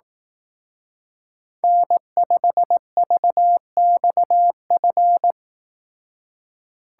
Morse code, heard as a keyed tone, about 18 wpm, 700 Hz. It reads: N5VXF
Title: Speed Racing - USA Call Signs 15wpm